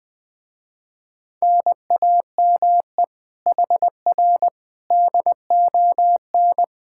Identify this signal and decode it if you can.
Morse code, heard as a keyed tone, about 20 wpm, 700 Hz. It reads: NAME HR DON
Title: Speed-Racing Common QSO Phrases Encore - 20wpm